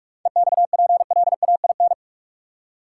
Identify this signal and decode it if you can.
Morse code, heard as a keyed tone, about 47 wpm, 700 Hz. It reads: EXPLAIN
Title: Seven Letter Words - 45wpm